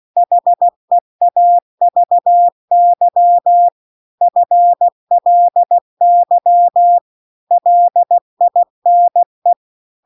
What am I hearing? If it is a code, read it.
Morse code, heard as a keyed tone, about 16 wpm, 700 Hz. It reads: HEAVY FLY LINE